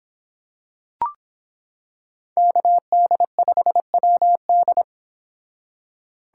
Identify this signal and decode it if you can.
Morse code, heard as a keyed tone, about 26 wpm, 700 Hz. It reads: KD5WD